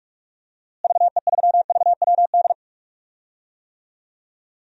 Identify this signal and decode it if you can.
Morse code, heard as a keyed tone, about 46 wpm, 700 Hz. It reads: VE3VWD
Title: DXCC-100 Call Signs - 6 Characters - Repeated 3x - 45wpm